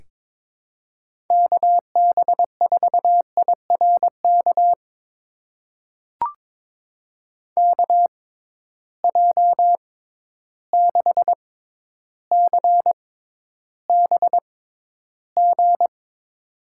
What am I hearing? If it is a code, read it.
Morse code, heard as a keyed tone, about 22 wpm, spaced as for 11 wpm, 700 Hz. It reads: KB4IRK KJ6CBG
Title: USA Call Signs - 6 Characters - 6x Character Spacing - 22wpm